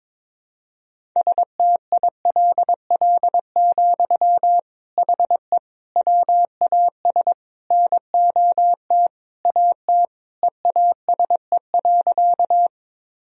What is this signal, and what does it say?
Morse code, heard as a keyed tone, about 22 wpm, 700 Hz. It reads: STILL, HE WAS NOT AT EASE.
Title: Sentences from Top 1000 Words - Review Part 1 - 22wpm